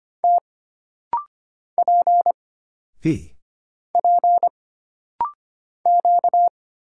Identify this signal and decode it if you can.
Morse code, heard as a keyed tone, about 25 wpm, 700 Hz. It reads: T P P Q
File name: Single Character (Letter:Number:Punctuation:Prosign) - Rapid-Fire - 25wpm